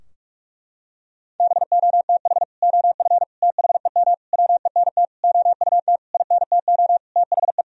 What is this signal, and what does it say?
Morse code, heard as a keyed tone, about 43 wpm, 700 Hz. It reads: BOTH OF THEM WENT OUT INTO THE
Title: Sentences from Top 1000 Words - Review Part 2 - 45wpm